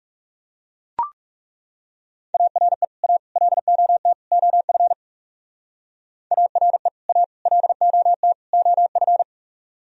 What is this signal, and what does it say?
Morse code, heard as a keyed tone, about 36 wpm, 700 Hz. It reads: ARE A LOT OF ARE A LOT OF
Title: Speed-Racing - Sets of 4 Words 30wpm